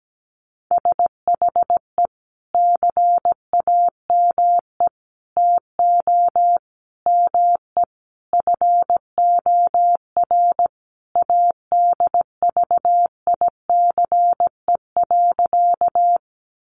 Morse code, heard as a keyed tone, about 17 wpm, 700 Hz. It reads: SHE CAME TO ME FOR ADVICE.
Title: Sentences from Top 1100 Words 17wpm